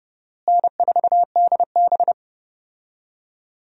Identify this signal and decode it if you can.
Morse code, heard as a keyed tone, about 30 wpm, 700 Hz. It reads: N4DB
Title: USA Call Signs - 4 Characters - Repeated 2x - 30wpm